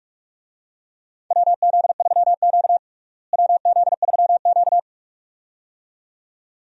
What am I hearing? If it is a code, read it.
Morse code, heard as a keyed tone, about 45 wpm, 700 Hz. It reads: WZ3Q WZ3Q
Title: USA Call Signs - 4 Characters - Repeated 2x - 45wpm